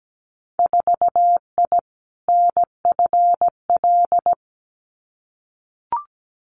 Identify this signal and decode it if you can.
Morse code, heard as a keyed tone, about 17 wpm, 700 Hz. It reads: 4I NFL
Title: Winter Field Day Exchanges 17wpm